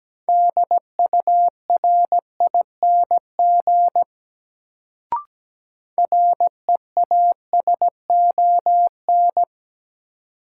Morse code, heard as a keyed tone, about 17 wpm, 700 Hz. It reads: DURING REASON
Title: Sentences from Top 700 Words - New Word Drill - 17wpm